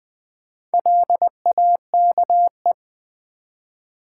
Morse code, heard as a keyed tone, about 20 wpm, 700 Hz. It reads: LAKE
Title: Sentences from Top 1100 Words - New Word Drill Alt v2 - 20wpm